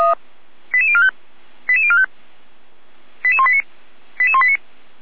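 Five-tone selective call (ZVEI), heard as quick runs of five tones, tones of about 70 ms. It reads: 88045 88045 88189 88189
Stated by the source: ZVEI five-tone selective call 88352 received on SDR.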